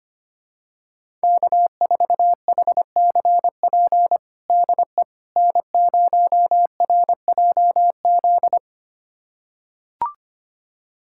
Morse code, heard as a keyed tone, about 25 wpm, 700 Hz. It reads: K4HCP DE N0RJZ
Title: USA Callsign DE Callsign - 5 Characters - 25wpm